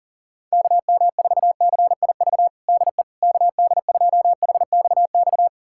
Morse code, heard as a keyed tone, about 37 wpm, 700 Hz. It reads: KM4CIV DE KD2HXX
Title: USA Callsign DE Callsign 40wpm